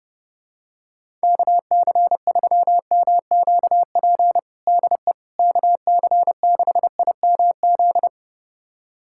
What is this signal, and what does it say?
Morse code, heard as a keyed tone, about 32 wpm, 700 Hz. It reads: KC3MQP DE KC6IMZ
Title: USA Callsign DE Callsign 30wpm